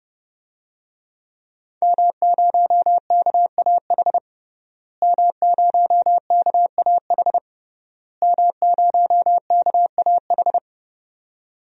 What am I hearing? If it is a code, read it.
Morse code, heard as a keyed tone, about 30 wpm, 700 Hz. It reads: M0KAH M0KAH M0KAH
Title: DXCC-100 Call Signs - 5 Characters - Repeated 3x - 30wpm